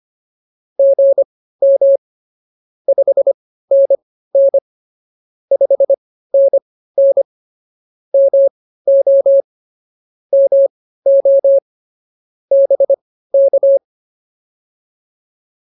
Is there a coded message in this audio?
Yes: Morse code from 0.8 to 13.8 s, heard as a keyed tone, about 24 wpm, spaced as for 15 wpm, 550 Hz. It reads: GM 5NN 5NN MO MO BK